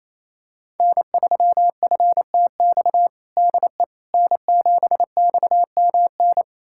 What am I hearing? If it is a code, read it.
Morse code, heard as a keyed tone, about 27 wpm, 700 Hz. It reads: N3FTX DE N7XMN
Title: USA Callsign DE Callsign - 5 Characters - 28wpm